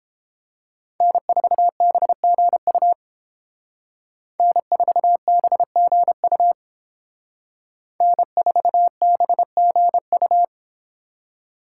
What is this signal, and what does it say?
Morse code, heard as a keyed tone, about 31 wpm, 700 Hz. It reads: N4BGU N4BGU N4BGU